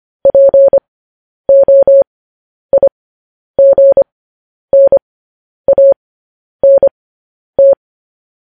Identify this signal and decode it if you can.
Morse code, heard as a keyed tone, about 25 wpm, spaced as for 10 wpm, 550 Hz. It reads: POIGNANT
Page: News Headlines in Morse Code at 10 WPM